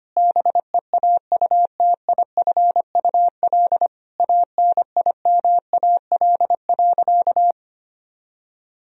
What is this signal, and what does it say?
Morse code, heard as a keyed tone, about 25 wpm, 700 Hz. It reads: BEAUTIFUL ANIMAL.
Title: Sentences from Top 1000 Words - Review Part 1 - 25wpm